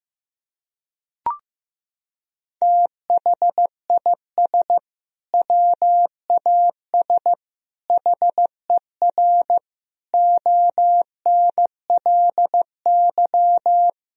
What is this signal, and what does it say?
Morse code, heard as a keyed tone, about 15 wpm, 700 Hz. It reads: THIS WAS HER ONLY
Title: Sentences from Top 1000 Words - Review Part 3 - 15wpm